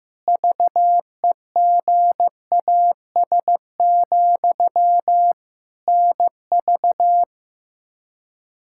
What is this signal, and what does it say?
Morse code, heard as a keyed tone, about 15 wpm, 700 Hz. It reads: VEGAS, NV
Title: Practice QSO - Exchange 1 - Beginning - 15wpm